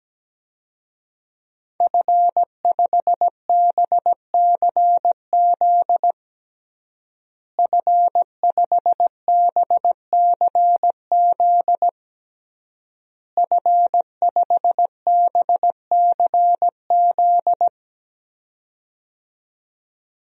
Morse code, heard as a keyed tone, about 17 wpm, 700 Hz. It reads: F5BCZ F5BCZ F5BCZ
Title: DXCC-100 Call Signs - 5 Characters - Repeated 3x - 17wpm